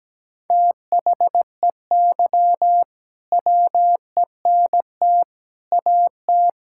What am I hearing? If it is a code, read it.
Morse code, heard as a keyed tone, about 17 wpm, 700 Hz. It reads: THEY WENT AT
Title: Sentences from Top 1000 Words - Review Part 5 - 17wpm